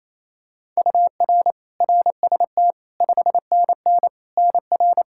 Morse code, heard as a keyed tone, about 28 wpm, 700 Hz. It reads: UR RST 5NN NR
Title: Practice QSO - Exchange 1 - Beginning - 28wpm